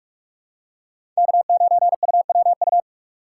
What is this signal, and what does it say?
Morse code, heard as a keyed tone, about 41 wpm, 700 Hz. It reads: K9UWU
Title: USA Call Signs - 5 Characters - 2x Character Spacing - 45wpm